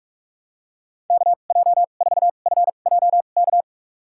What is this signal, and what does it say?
Morse code, heard as a keyed tone, about 46 wpm, spaced as for 32 wpm, 700 Hz. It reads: KJ4FJK
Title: USA Call Signs - 6 Characters - 2x Character Spacing - 45wpm